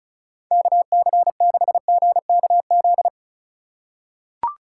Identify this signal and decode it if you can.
Morse code, heard as a keyed tone, about 36 wpm, 700 Hz. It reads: KC6GKZ
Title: USA Call Signs - 6 Characters - 3x Character Spacing - 35wpm